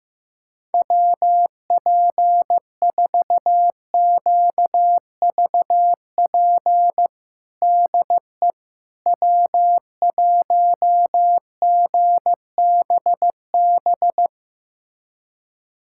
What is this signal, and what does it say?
Morse code, heard as a keyed tone, about 15 wpm, 700 Hz. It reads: WP4QVP DE W1GBB
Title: USA Callsign DE Callsign 15wpm